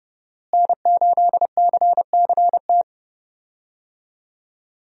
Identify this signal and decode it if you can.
Morse code, heard as a keyed tone, about 30 wpm, 700 Hz. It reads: N8CCT
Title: USA Call Signs - 5 Characters - Repeated 3x - 30wpm